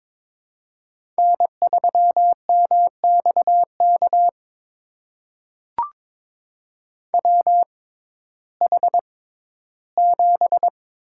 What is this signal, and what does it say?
Morse code, heard as a keyed tone, about 21 wpm, spaced as for 13 wpm, 700 Hz. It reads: N3MXK WH7
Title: USA Call Signs - 5 Characters - 6x Character Spacing - 22wpm